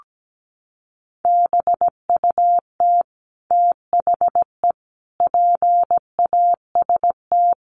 Morse code, heard as a keyed tone, about 17 wpm, 700 Hz. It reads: BUT THE PAST
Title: Sentences from Top 1200 Words 17wpm